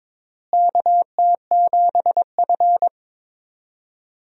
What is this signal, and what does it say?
Morse code, heard as a keyed tone, about 22 wpm, 700 Hz. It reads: KT7F
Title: USA Call Signs - 4 Characters - Repeated 2x - 22wpm